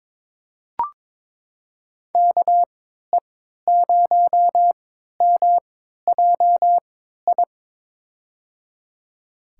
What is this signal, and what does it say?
Morse code, heard as a keyed tone, about 22 wpm, spaced as for 13 wpm, 700 Hz. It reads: KE0MJI
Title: USA Call Signs - 6 Characters - 3x Character Spacing - 22wpm